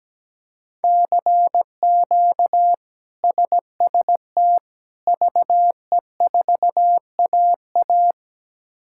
Morse code, heard as a keyed tone, about 17 wpm, 700 Hz. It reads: CQ SST VE4AA
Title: K1USN SST QSO Practice - 17wpm